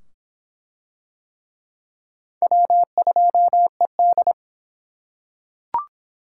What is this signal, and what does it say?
Morse code, heard as a keyed tone, about 24 wpm, 700 Hz. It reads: W2ED